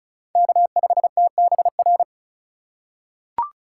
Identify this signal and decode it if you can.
Morse code, heard as a keyed tone, about 36 wpm, 700 Hz. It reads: K5TBR
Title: USA Call Signs - 8x Character Spacing - 35wpm